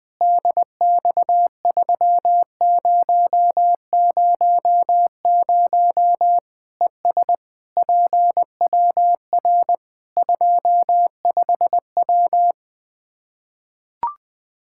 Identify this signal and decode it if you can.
Morse code, heard as a keyed tone, about 21 wpm, 700 Hz. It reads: DX3000 ES PWR 25W
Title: Practice QSO - Exchange 2 - Info - 20wpm